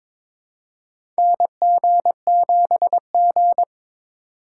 Morse code, heard as a keyed tone, about 21 wpm, 700 Hz. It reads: NG7G
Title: USA Call Signs - 4 Characters - 4x Character Spacing - 22wpm